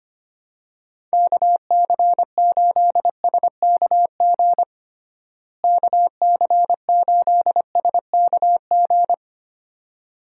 Morse code, heard as a keyed tone, about 25 wpm, 700 Hz. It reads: KC8SKG KC8SKG
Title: USA Call Signs - 6 Characters - Repeated 2x - 25wpm